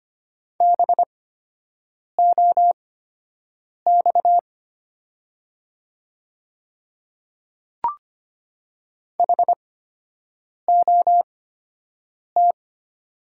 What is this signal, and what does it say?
Morse code, heard as a keyed tone, about 25 wpm, spaced as for 7 wpm, 700 Hz. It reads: BOX HOT
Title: 8x Character Spacing - Top 500 Words - 25wpm